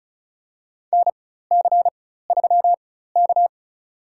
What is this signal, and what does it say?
Morse code, heard as a keyed tone, about 34 wpm, spaced as for 16 wpm, 700 Hz. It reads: NC3K
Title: USA Call Signs - 4 Characters - 4x Character Spacing - 35wpm